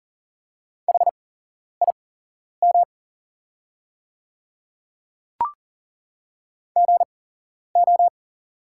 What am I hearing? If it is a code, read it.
Morse code, heard as a keyed tone, about 40 wpm, spaced as for 11 wpm, 690 Hz. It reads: HIM GO
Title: 8x Character Spacing - Top 100 Words - 40wpm